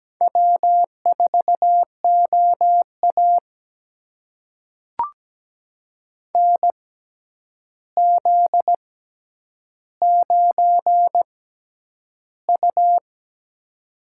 Morse code, heard as a keyed tone, about 17 wpm, spaced as for 9 wpm, 700 Hz. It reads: W4OA NZ9U
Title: USA Call Signs - 4 Characters - 6x Character Spacing - 17wpm